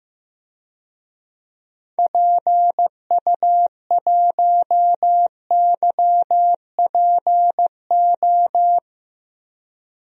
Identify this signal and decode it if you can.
Morse code, heard as a keyed tone, about 15 wpm, 700 Hz. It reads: PU1YPO
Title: DXCC-100 Call Signs - 6 Characters - 15wpm